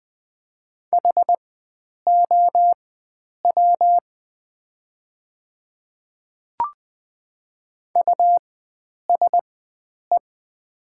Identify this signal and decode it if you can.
Morse code, heard as a keyed tone, about 20 wpm, spaced as for 9 wpm, 700 Hz. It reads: HOW USE
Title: 4x Character Spacing - Top 100 Words - 20wpm